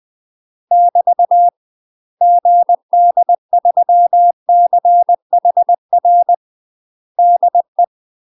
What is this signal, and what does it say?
Morse code, heard as a keyed tone, about 20 wpm, 700 Hz. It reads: <BT> GD3CHR DE
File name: UK MASTER QSOs TRAINING / 20WPM / UK MASTER QSOs 0034